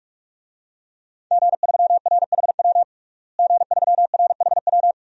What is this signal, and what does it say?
Morse code, heard as a keyed tone, about 46 wpm, 700 Hz. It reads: G3RHW G3RHW
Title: DXCC-100 Call Signs - 5 Characters - Repeated 3x - 45wpm